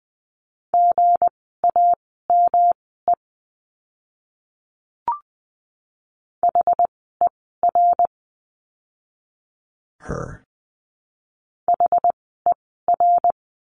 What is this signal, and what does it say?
Morse code, heard as a keyed tone, about 20 wpm, spaced as for 15 wpm, 700 Hz. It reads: GAME HER HER
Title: 160 Lesson 26f - Letter g - Words 20wpm